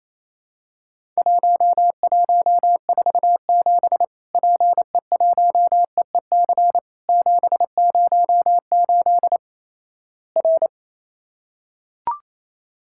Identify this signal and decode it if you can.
Morse code, heard as a keyed tone, about 27 wpm, 700 Hz. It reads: 1147 PE1EEC 708 R
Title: Worked All Europe DX Contest - QTC Exchange - 28wpm